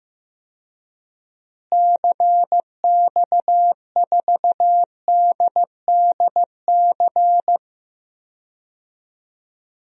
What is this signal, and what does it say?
Morse code, heard as a keyed tone, about 15 wpm, 700 Hz. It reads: CX4DDC
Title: DXCC-100 Call Signs - 6 Characters - Repeated 3x - 15wpm